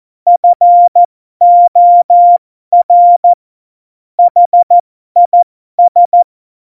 Morse code, heard as a keyed tone, about 14 wpm, spaced as for 12 wpm, 700 Hz. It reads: FOR HIS
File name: Morsekurs / Witze